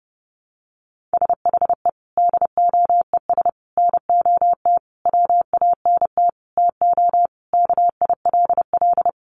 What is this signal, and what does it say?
Morse code, heard as a keyed tone, about 28 wpm, 700 Hz. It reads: SHE DOES NOT WANT TO KILL
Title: Additional Easy Phrases 30wpm